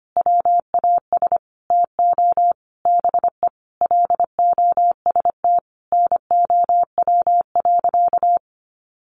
Morse code, heard as a keyed tone, about 25 wpm, 700 Hz. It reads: WAS TO BE LOST NOW.
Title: Sentences from Top 1400 Words 25wpm